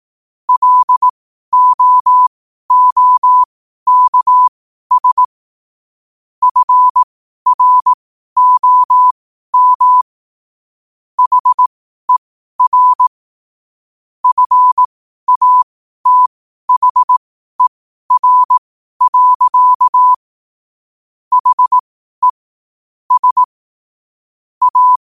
Morse code, heard as a keyed tone, about 18 wpm, spaced as for 12 wpm, 1000 Hz. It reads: LOOKS FROM HER FATHER. HE S A
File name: qotdfiles / 12wpm